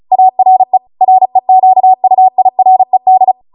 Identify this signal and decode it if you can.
Morse code, heard as a keyed tone, about 35 wpm, 750 Hz. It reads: ARE REQUIRED
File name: Morse / 35WPM